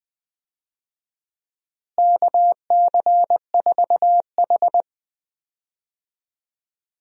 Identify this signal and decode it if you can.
Morse code, heard as a keyed tone, about 20 wpm, 700 Hz. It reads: KC4H
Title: USA Call Signs - 4 Characters - Repeated 2x - 20wpm